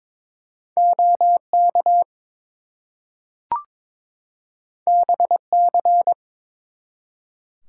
Morse code, heard as a keyed tone, about 22 wpm, 700 Hz. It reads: OK BC
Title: US States and Canadian Provinces 22wpm